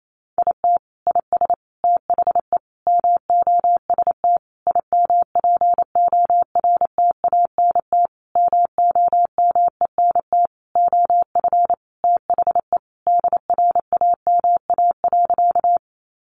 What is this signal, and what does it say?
Morse code, heard as a keyed tone, about 28 wpm, 700 Hz. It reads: IT IS THE MOST IMPORTANT MOMENT OF THE DRAMA.
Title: Sentences from Top 1500 Words 28wpm